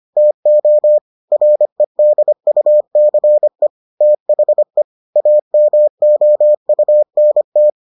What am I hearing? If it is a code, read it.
Morse code, heard as a keyed tone, about 25 wpm, 600 Hz. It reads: TO REDUCE THE AMOUNT